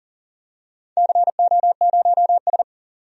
Morse code, heard as a keyed tone, about 40 wpm, 700 Hz. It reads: CO0S
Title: DXCC-100 Call Signs - 4 Characters - 40wpm